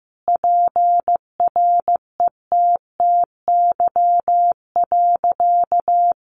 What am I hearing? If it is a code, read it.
Morse code, heard as a keyed tone, about 15 wpm, 700 Hz. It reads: PRETTY.
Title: Sentences from Top 1000 Words 15wpm